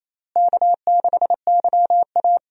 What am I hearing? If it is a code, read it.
Morse code, heard as a keyed tone, about 28 wpm, 700 Hz. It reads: K6YA
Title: USA Call Signs - 4 Characters - Repeated 2x - 28wpm